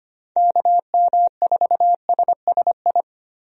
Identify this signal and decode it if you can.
Morse code, heard as a keyed tone, about 25 wpm, 700 Hz. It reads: KM4SSI
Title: USA Call Signs - 6 Characters - 25wpm